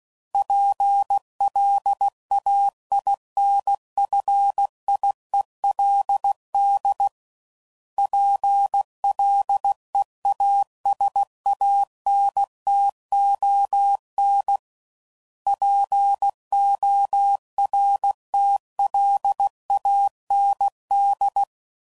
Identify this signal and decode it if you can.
Morse code, heard as a keyed tone, about 16 wpm, 800 Hz. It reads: PLAINFIELD PLEASANTON PORTLAND
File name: CW Code Groups @15WPM / fromSM's / Cities